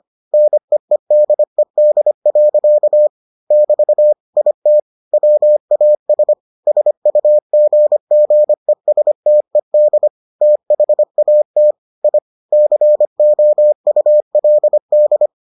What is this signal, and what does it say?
Morse code, heard as a keyed tone, about 25 wpm, 600 Hz. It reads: NEEDED. <BT> IT WAS SUGGESTED THAT I COULD